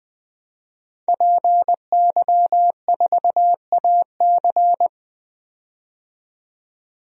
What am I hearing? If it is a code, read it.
Morse code, heard as a keyed tone, about 20 wpm, 700 Hz. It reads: PY4AC
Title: DXCC-100 Call Signs - 5 Characters - 20wpm